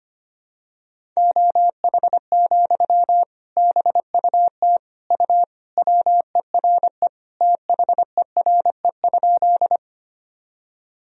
Morse code, heard as a keyed tone, about 25 wpm, 700 Hz. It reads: OH, BUT U WERE THERE?
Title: Sentences from Top 500 Words and Abbreviations 25wpm